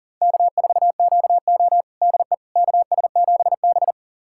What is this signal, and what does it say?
Morse code, heard as a keyed tone, about 40 wpm, 700 Hz. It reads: K4QO DE KS7B
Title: USA Callsign DE Callsign - 4 Characters - 40wpm